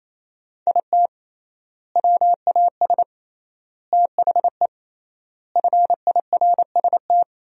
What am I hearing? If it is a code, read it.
Morse code, heard as a keyed tone, about 28 wpm, 700 Hz. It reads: IT WAS THE FIRST
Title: 3x Word-Spacing - Sentences from Top 300 Words - 28wpm